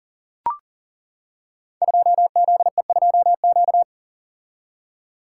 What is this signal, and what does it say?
Morse code, heard as a keyed tone, about 39 wpm, 700 Hz. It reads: 2ZE2Q
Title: Code Groups - 5 Letter:Numbers - 40wpm